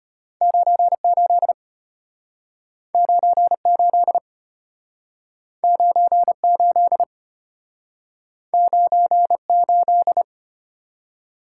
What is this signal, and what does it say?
Morse code, heard as a keyed tone, about 34 wpm, 700 Hz. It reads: 98 98 98 98